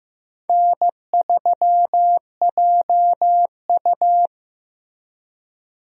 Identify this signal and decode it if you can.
Morse code, heard as a keyed tone, about 15 wpm, 700 Hz. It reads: N3JU
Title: USA Call Signs - 4 Characters - Repeated 3x - 15wpm